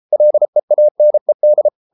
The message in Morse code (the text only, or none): LEANED